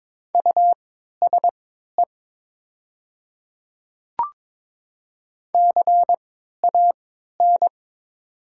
Morse code, heard as a keyed tone, about 22 wpm, spaced as for 12 wpm, 700 Hz. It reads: USE CAN
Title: 3x Character Spacing - Top 100 Words - 22wpm